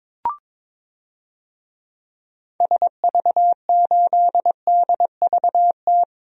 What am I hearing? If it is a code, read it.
Morse code, heard as a keyed tone, about 22 wpm, 700 Hz. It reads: SV8DVT
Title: DXCC-100 Call Signs - 6 Characters - Repeated 3x - 22wpm